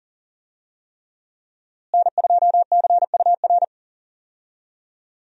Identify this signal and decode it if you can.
Morse code, heard as a keyed tone, about 39 wpm, 700 Hz. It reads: N2CUR
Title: USA Call Signs - 5 Characters - Repeated 3x - 40wpm